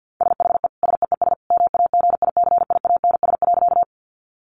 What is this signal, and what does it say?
Morse code, heard as a keyed tone, about 50 wpm, 700 Hz. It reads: SHE SEES MAGICIANS.
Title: Easy Phrases 50wpm